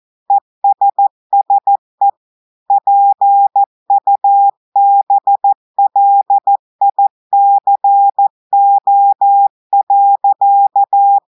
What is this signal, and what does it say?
Morse code, heard as a keyed tone, about 14 wpm, 800 Hz. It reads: ESSE PUBLICO.
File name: audios / portugais / portugais texte en clair 2 vitesse 14 mots minutes